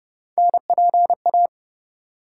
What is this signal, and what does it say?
Morse code, heard as a keyed tone, about 29 wpm, 700 Hz. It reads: NPA